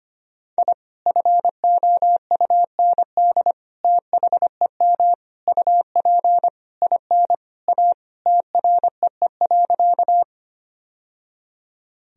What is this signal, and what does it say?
Morse code, heard as a keyed tone, about 25 wpm, 700 Hz. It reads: I FOUND THEM UP IN A TREE.
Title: Sentences from Top 1000 Words - Review Part 1 - 25wpm